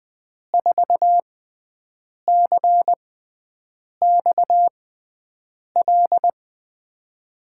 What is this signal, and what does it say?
Morse code, heard as a keyed tone, about 20 wpm, spaced as for 7 wpm, 700 Hz. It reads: 4CXL
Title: USA Call Signs - 6 Characters - 6x Character Spacing - 20wpm